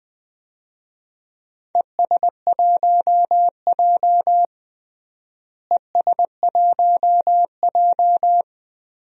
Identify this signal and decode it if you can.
Morse code, heard as a keyed tone, about 19 wpm, 690 Hz. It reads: ES1J ES1J
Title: DXCC-100 Call Signs - 4 Characters - Repeated 3x - 20wpm